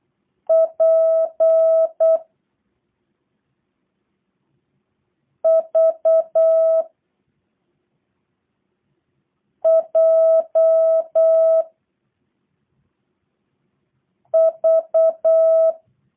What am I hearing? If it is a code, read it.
Morse code, heard as a keyed tone, about 8 wpm, 650 Hz. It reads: P V J V
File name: jh8cbh / 2023cw / review1-5